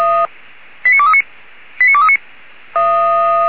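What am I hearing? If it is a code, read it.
Five-tone selective call (ZVEI), heard as quick runs of five tones, tones of about 80 ms. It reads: 89239 89239
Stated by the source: ZVEI five-tone selective call received by SDR.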